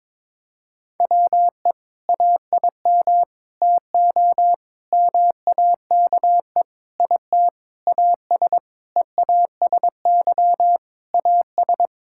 Morse code, heard as a keyed tone, about 22 wpm, 700 Hz. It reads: WE AIM TO MAKE IT AS EASY AS
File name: Sentences with 7-Word Phrases - 22wpm